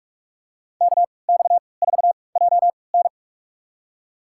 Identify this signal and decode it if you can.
Morse code, heard as a keyed tone, about 46 wpm, spaced as for 25 wpm, 700 Hz. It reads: KX4JN
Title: USA Call Signs - 5 Characters - 3x Character Spacing - 45wpm